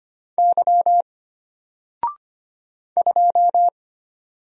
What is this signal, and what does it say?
Morse code, heard as a keyed tone, about 25 wpm, 700 Hz. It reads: Y 2
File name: Single-Letter-Number-25wpm